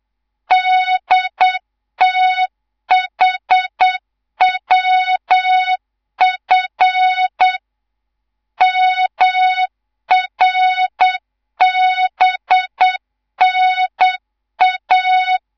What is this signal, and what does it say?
Morse code, heard as a keyed tone, about 8 wpm, 3700 Hz, with noise. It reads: DTHWF MRBNA
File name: Morze-HA2MN / hang / 11 GYAKORLAT